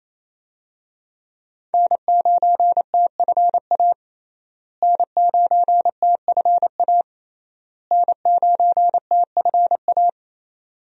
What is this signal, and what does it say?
Morse code, heard as a keyed tone, about 27 wpm, 700 Hz. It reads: N9TFA N9TFA N9TFA
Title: USA Call Signs - 5 Characters - Repeated 3x - 28wpm